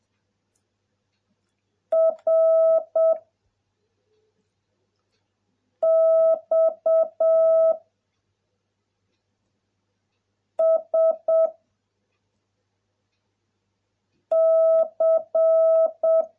Morse code, heard as a keyed tone, about 7 wpm, 660 Hz. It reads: R X S C